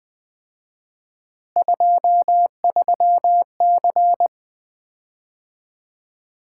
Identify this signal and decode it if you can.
Morse code, heard as a keyed tone, about 20 wpm, 700 Hz. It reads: 23C
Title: Common QSO Phrases - 4x Word-Spacing - 20wpm